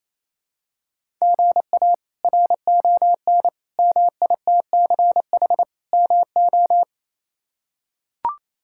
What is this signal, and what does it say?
Morse code, heard as a keyed tone, about 28 wpm, 700 Hz. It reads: GA RON MITCH MO
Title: K1USN SST QSO Practice - 28wpm